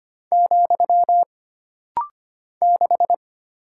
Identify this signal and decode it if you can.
Morse code, heard as a keyed tone, about 25 wpm, 700 Hz. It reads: , 6
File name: Single Character (Letter:Number:Punctuation:Prosign) - Rapid-Fire - 25wpm